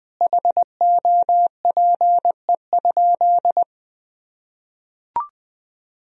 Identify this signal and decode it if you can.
Morse code, heard as a keyed tone, about 20 wpm, 700 Hz. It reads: HOPE?
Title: Sentences from Top 1000 Words - Review Part 3 - 20wpm